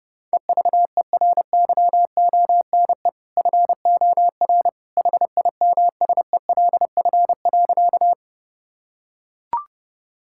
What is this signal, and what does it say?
Morse code, heard as a keyed tone, about 30 wpm, 700 Hz. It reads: EVERYONE FOR HIMSELF.
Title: Sentences from Top 1600 Words 30wpm